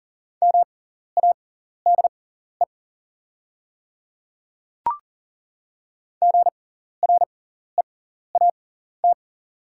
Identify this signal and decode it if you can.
Morse code, heard as a keyed tone, about 39 wpm, spaced as for 14 wpm, 700 Hz. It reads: MADE GREAT
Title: 6x Character Spacing - Top 200 Words - 40wpm